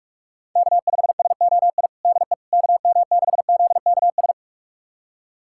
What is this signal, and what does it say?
Morse code, heard as a keyed tone, about 46 wpm, 700 Hz. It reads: K5SOI DE KM6ZKS
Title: USA Callsign DE Callsign 45wpm